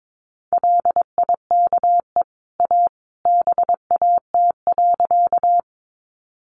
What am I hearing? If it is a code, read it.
Morse code, heard as a keyed tone, about 22 wpm, 700 Hz. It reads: LIKE A BAT.
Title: Sentences from Top 1500 Words 22wpm